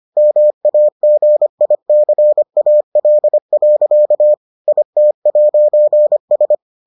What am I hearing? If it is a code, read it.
Morse code, heard as a keyed tone, about 25 wpm, 600 Hz. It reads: MAGICAL. IT'S